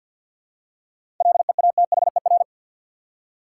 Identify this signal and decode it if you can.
Morse code, heard as a keyed tone, about 48 wpm, 700 Hz. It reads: LEATHER
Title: Seven Letter Words - 50wpm